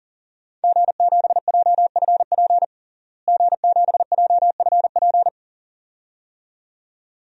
Morse code, heard as a keyed tone, about 40 wpm, 700 Hz. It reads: G7JFP G7JFP
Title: DXCC-100 Call Signs - 5 Characters - Repeated 2x - 40wpm